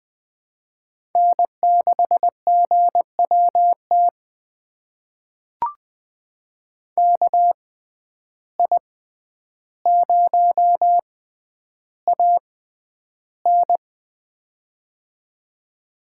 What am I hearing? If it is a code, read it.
Morse code, heard as a keyed tone, about 20 wpm, spaced as for 10 wpm, 700 Hz. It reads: N6GWT KI0AN
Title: USA Call Signs - 5 Characters - 6x Character Spacing - 20wpm